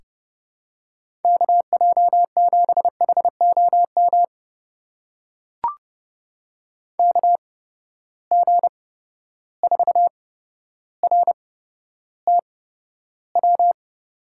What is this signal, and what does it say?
Morse code, heard as a keyed tone, about 30 wpm, spaced as for 13 wpm, 700 Hz. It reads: KJ7HOM KG4RTW
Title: USA Call Signs - 6 Characters - 8x Character Spacing - 30wpm